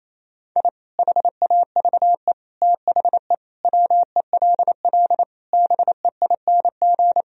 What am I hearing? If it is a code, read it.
Morse code, heard as a keyed tone, about 27 wpm, 700 Hz. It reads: I HAVE THE WELL BEING
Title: Sentences from Top 1000 Words - Review Part 3 - 28wpm